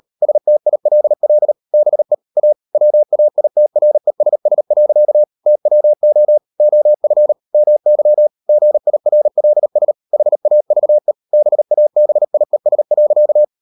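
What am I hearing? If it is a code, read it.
Morse code, heard as a keyed tone, about 38 wpm, 600 Hz. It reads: STILL BE A WAITRESS. TWO OF MY GIRLS HAVE BABIES.
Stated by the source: Morse code practice at 38 WPM, sending Ken Follett's novel Never in English, from CW med Gnister.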